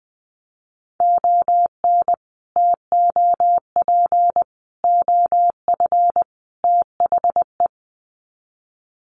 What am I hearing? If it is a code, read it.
Morse code, heard as a keyed tone, about 20 wpm, 700 Hz. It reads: ON TOP OF THE
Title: Sets of 4 Words 20wpm